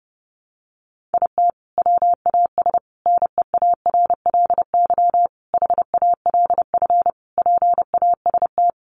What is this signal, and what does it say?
Morse code, heard as a keyed tone, about 30 wpm, 700 Hz. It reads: IT WAS NEARLY HALF PAST
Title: Sentences from Top 1300 Words 30wpm